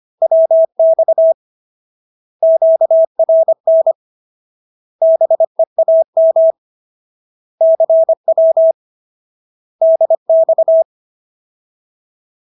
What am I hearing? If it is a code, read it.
Morse code, heard as a keyed tone, about 25 wpm, 650 Hz. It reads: WX QRN BEAM CW DX